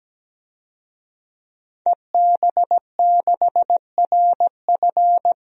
Morse code, heard as a keyed tone, about 17 wpm, 700 Hz. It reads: EB6RF
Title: DXCC-100 Call Signs - 5 Characters - Repeated 3x - 17wpm